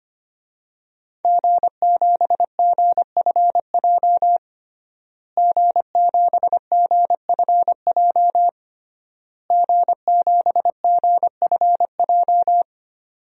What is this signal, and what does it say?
Morse code, heard as a keyed tone, about 25 wpm, 700 Hz. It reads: G7GFJ G7GFJ G7GFJ
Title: DXCC-100 Call Signs - 5 Characters - Repeated 3x - 25wpm